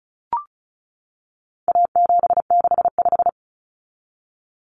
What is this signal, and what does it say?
Morse code, heard as a keyed tone, about 34 wpm, 700 Hz. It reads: A765
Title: Common Cut Numbers - 4-Digit 35wpm